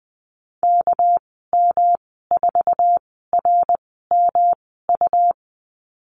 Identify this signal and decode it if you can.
Morse code, heard as a keyed tone, about 20 wpm, spaced as for 15 wpm, 700 Hz. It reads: KM4RMU
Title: 238 Lesson 38g - Letter q - Call Signs 20wpm